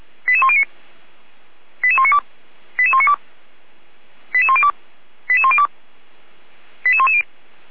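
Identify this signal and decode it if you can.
Five-tone selective call (ZVEI), heard as quick runs of five tones, tones of about 70 ms. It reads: 88109 88192 88192 88102 88102 88110
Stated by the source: ZVEI five-tone selective call 88104, received on an SDR, from the Feuerwehr Orlamünde (fire brigade) audio server.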